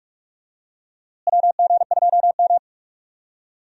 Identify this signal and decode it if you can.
Morse code, heard as a keyed tone, about 44 wpm, 700 Hz. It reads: WG2M
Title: USA Call Signs - 4 Characters - Repeated 2x - 45wpm